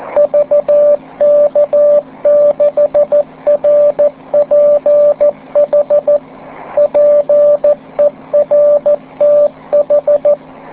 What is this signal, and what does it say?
Morse code, heard as a keyed tone, about 14 wpm, 590 Hz, with radio noise. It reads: VK6RPH PERTH